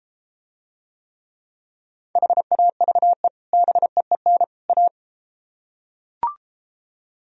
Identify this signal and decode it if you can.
Morse code, heard as a keyed tone, about 33 wpm, 700 Hz. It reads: HAVE BEEN A